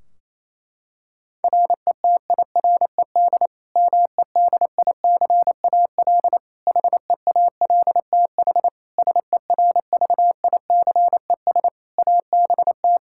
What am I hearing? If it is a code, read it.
Morse code, heard as a keyed tone, about 27 wpm, 700 Hz. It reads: RETIRED MEDICAL HEALTH SERVICES ABT
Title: K1USN SST QSO Practice 28wpm